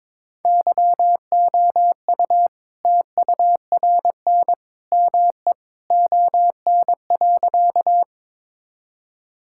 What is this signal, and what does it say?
Morse code, heard as a keyed tone, about 22 wpm, 690 Hz. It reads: YOU TURN ME ON.
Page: morse-code-ninja-practice-22wpm